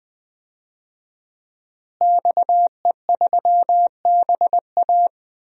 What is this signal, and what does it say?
Morse code, heard as a keyed tone, about 20 wpm, 700 Hz. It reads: XE3BA
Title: DXCC-100 Call Signs - 5 Characters - Repeated 3x - 20wpm